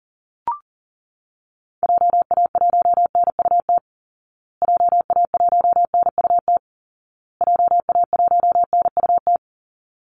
Morse code, heard as a keyed tone, about 40 wpm, 700 Hz. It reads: JA1NUT JA1NUT JA1NUT
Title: Call Signs Repeated 2x - 40wpm